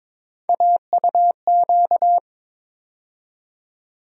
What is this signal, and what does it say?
Morse code, heard as a keyed tone, about 22 wpm, 700 Hz. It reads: AUQ
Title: USA Call Sign Postfixes - 3 Letters 22wpm